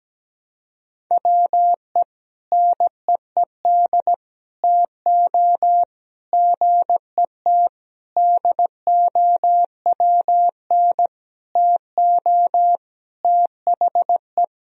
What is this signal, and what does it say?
Morse code, heard as a keyed tone, about 17 wpm, 700 Hz. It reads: WE NEED TO GET DOWN TO THE
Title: Sentences from Top 1000 Words - Review Part 1 - 17wpm